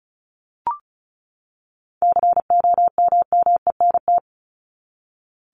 Morse code, heard as a keyed tone, about 34 wpm, 700 Hz. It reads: COMMENT
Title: Top 1751-2000 Words 35wpm